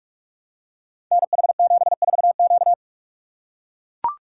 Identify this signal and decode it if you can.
Morse code, heard as a keyed tone, about 44 wpm, 690 Hz. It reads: NH74Q
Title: Code Groups - 5 Letter:Numbers - 45wpm